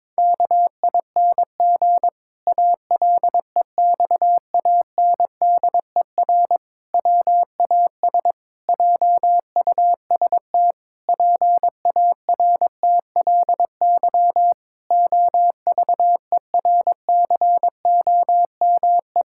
Morse code, heard as a keyed tone, about 22 wpm, 700 Hz. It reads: KING ALEXANDER WAS JUST PARTLY OVERCOME
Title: Pangram Challenge v2 - 22wpm